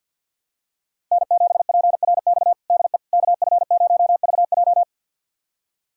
Morse code, heard as a keyed tone, about 51 wpm, 700 Hz. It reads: N7PRX DE KF0VJ
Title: USA Callsign DE Callsign - 5 Characters - 50wpm